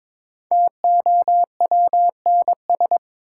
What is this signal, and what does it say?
Morse code, heard as a keyed tone, about 22 wpm, 700 Hz. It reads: TOWNS
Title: Sentences from Top 1000 Words - Review Part 6 - New Word Drill - 22wpm